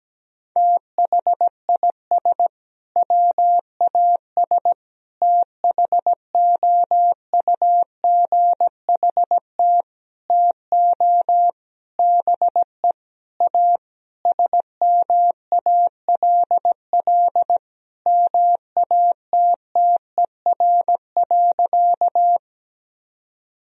Morse code, heard as a keyed tone, about 17 wpm, 700 Hz. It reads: THIS WAS THOUGHT TO BE A SMALL MATTER.
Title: Sentences from Top 1000 Words - Review Part 1 - 17wpm